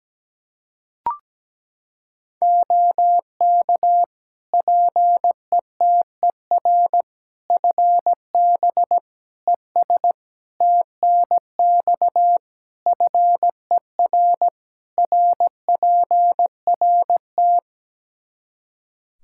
Morse code, heard as a keyed tone, about 17 wpm, 700 Hz. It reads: OK PETER FB ES TNX FER RPRT
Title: K1USN SST QSO Practice 17wpm